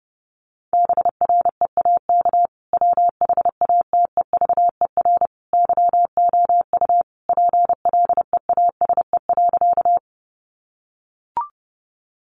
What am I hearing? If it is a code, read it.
Morse code, heard as a keyed tone, about 30 wpm, 700 Hz. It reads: BREAK WHATEVER YOU PLEASE.
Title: Sentences from Top 1500 Words 30wpm